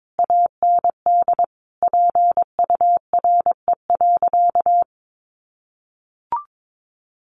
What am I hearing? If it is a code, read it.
Morse code, heard as a keyed tone, about 22 wpm, 700 Hz. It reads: AND PURE.
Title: Sentences from Top 1500 Words 22wpm